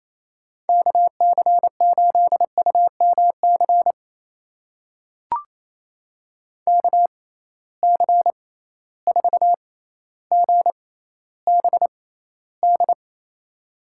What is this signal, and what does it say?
Morse code, heard as a keyed tone, about 28 wpm, spaced as for 14 wpm, 700 Hz. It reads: KC8UMC KC4GBD
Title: USA Call Signs - 6 Characters - 6x Character Spacing - 28wpm